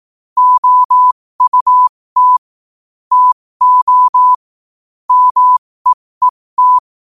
Morse code, heard as a keyed tone, about 18 wpm, spaced as for 15 wpm, 1000 Hz. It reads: OUT TO MEET
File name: qotdfiles / 15wpm